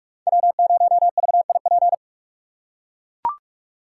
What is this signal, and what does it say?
Morse code, heard as a keyed tone, about 44 wpm, 700 Hz. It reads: W0VIP